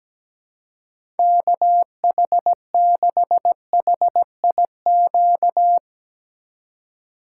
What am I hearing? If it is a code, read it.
Morse code, heard as a keyed tone, about 17 wpm, 700 Hz. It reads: KH6HIQ
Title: USA Call Signs - 6 Characters - 2x Character Spacing - 17wpm